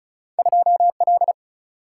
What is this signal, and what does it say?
Morse code, heard as a keyed tone, about 36 wpm, 700 Hz. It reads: 2 L
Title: Single-Letter Single-Number 35wpm